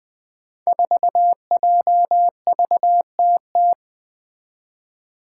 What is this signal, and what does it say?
Morse code, heard as a keyed tone, about 20 wpm, 700 Hz. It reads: 4JVTT
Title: Five Challenging Letter-Number Combinations 20wpm